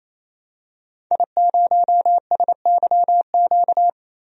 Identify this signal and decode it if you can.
Morse code, heard as a keyed tone, about 28 wpm, 700 Hz. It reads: I0SYQ
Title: DXCC-100 Call Signs - 5 Characters - Repeated 3x - 28wpm